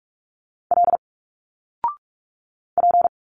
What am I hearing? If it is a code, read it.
Morse code, heard as a keyed tone, about 46 wpm, 700 Hz. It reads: L P
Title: Single Letters 45wpm